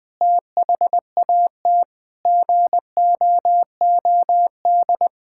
Morse code, heard as a keyed tone, about 20 wpm, 700 Hz. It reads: THAT GOOD
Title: Sentences from Top 1000 Words - Review Part 6 - 20wpm